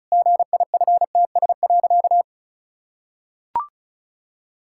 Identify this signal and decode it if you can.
Morse code, heard as a keyed tone, about 36 wpm, 700 Hz. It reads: GIFTS.
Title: Pangram Challenge v2 - 35wpm